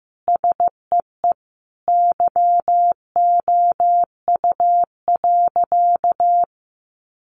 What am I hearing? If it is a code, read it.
Morse code, heard as a keyed tone, about 15 wpm, 700 Hz. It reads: SEE YOU.
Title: Sentences from Top 800 Words 15wpm